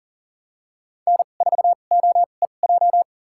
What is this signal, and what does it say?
Morse code, heard as a keyed tone, about 40 wpm, spaced as for 29 wpm, 700 Hz. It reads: N4OEJ
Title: USA Call Signs - 5 Characters - 2x Character Spacing - 40wpm